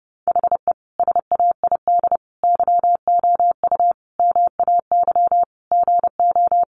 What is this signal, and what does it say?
Morse code, heard as a keyed tone, about 30 wpm, 700 Hz. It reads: HE SAID YOU MAY GO
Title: Sentences from Top 1300 Words 30wpm